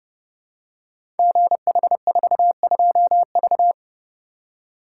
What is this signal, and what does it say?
Morse code, heard as a keyed tone, about 30 wpm, 700 Hz. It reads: GH42V
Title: Code Groups - 5 Letter:Numbers - 30wpm